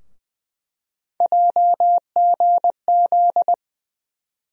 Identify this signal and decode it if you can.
Morse code, heard as a keyed tone, about 20 wpm, 700 Hz. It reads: JGZ
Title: USA Call Sign Postfixes - 3 Letters 20wpm